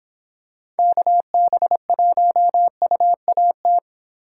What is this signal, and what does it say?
Morse code, heard as a keyed tone, about 26 wpm, 700 Hz. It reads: KB1UAT